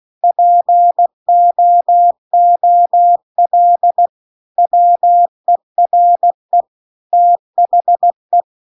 Morse code, heard as a keyed tone, about 16 wpm, 700 Hz. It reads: POOL WERE THE
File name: CW-The-Big-Twoheaded-River-1-16WPM